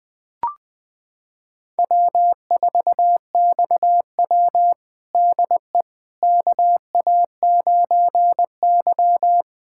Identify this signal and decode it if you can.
Morse code, heard as a keyed tone, about 20 wpm, 700 Hz. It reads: W4XW DE KA9Y
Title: USA Callsign DE Callsign - 4 Characters - 20wpm